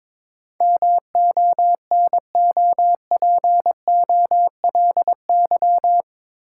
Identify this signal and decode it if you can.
Morse code, heard as a keyed tone, about 22 wpm, 700 Hz. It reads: MONOPOLY
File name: practice / 22wpm / Eight Letter Words - 22wpm